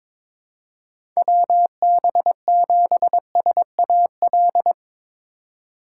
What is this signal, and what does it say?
Morse code, heard as a keyed tone, about 22 wpm, 700 Hz. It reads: WB7SAL
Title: USA Call Signs - 3x Character Spacing - 22wpm